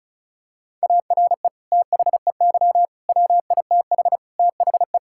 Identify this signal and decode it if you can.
Morse code, heard as a keyed tone, about 36 wpm, 700 Hz. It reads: ARE THEY WITH THE
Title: Sentences from Top 1000 Words - Review Part 5 - 35wpm